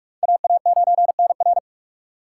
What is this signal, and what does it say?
Morse code, heard as a keyed tone, about 46 wpm, 700 Hz. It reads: AA9NR
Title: USA Call Signs - 3x Character Spacing - 45wpm